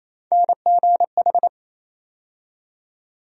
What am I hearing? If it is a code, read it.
Morse code, heard as a keyed tone, about 28 wpm, 700 Hz. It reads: NGH
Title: USA Call Sign Postfixes - 3 Letters 28wpm